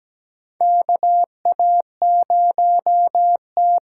Morse code, heard as a keyed tone, about 17 wpm, 700 Hz. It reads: KA0T
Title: USA Call Signs - 6 Characters - 8x Character Spacing - 17wpm